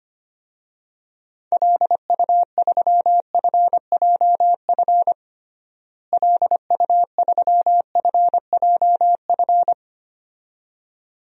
Morse code, heard as a keyed tone, about 25 wpm, 700 Hz. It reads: LU3FJF LU3FJF
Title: DXCC-100 Call Signs - 6 Characters - Repeated 2x - 25wpm